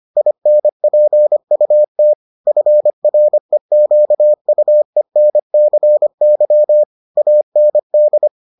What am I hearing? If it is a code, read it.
Morse code, heard as a keyed tone, about 25 wpm, 600 Hz. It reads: INPUT FREQUENCY AND